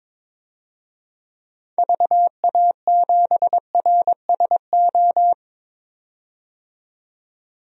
Morse code, heard as a keyed tone, about 22 wpm, 700 Hz. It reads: VA7RSO
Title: DXCC-100 Call Signs - 6 Characters - 22wpm